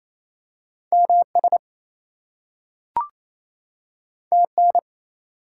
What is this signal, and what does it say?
Morse code, heard as a keyed tone, about 28 wpm, 700 Hz. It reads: MS TN
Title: US States and Canadian Provinces 28wpm